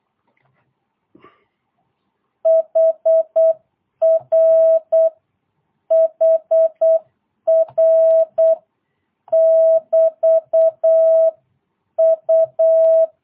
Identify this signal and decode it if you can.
Morse code, heard as a keyed tone, about 8 wpm, 650 Hz. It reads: HR HR <BT> U
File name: jh8cbh / 2022cw